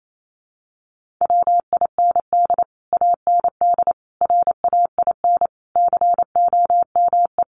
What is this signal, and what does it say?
Morse code, heard as a keyed tone, about 28 wpm, 700 Hz. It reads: WIND AND RAIN COME
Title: Sentences from Top 1400 Words 28wpm